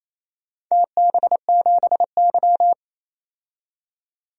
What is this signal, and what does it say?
Morse code, heard as a keyed tone, about 27 wpm, 700 Hz. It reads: TB7Y
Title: DXCC-100 Call Signs - 4 Characters - 28wpm